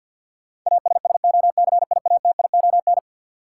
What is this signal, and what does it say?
Morse code, heard as a keyed tone, about 49 wpm, 700 Hz. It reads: ASSOCIATION
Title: Sentences from Top 1100 Words - New Word Drill Alt - 50wpm